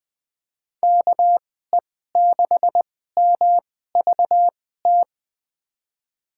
Morse code, heard as a keyed tone, about 21 wpm, spaced as for 15 wpm, 700 Hz. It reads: KE6MVT
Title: USA Call Signs - 6 Characters - 2x Character Spacing - 20wpm